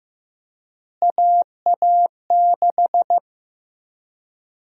Morse code, heard as a keyed tone, about 15 wpm, 700 Hz. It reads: AA6
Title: USA Call Sign Prefixes - Repeated 2x - 15wpm